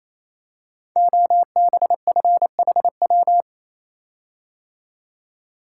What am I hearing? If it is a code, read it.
Morse code, heard as a keyed tone, about 28 wpm, 700 Hz. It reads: OBFHW
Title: Code Groups - 5 Letters - 28wpm